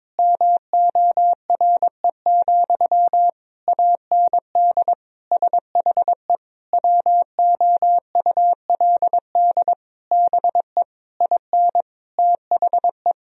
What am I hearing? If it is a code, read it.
Morse code, heard as a keyed tone, about 22 wpm, 700 Hz. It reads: MORE, AND SHE WOULD BE IN THE
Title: Sentences from Top 1000 Words - Review Part 3 - 22wpm